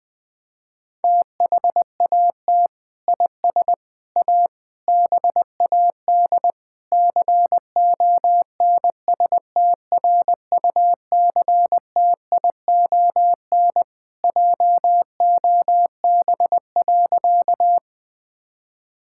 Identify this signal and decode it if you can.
Morse code, heard as a keyed tone, about 20 wpm, 700 Hz. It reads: THAT IS A BAD CONSTRUCTION JOB.
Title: Sentences from Top 1600 Words 20wpm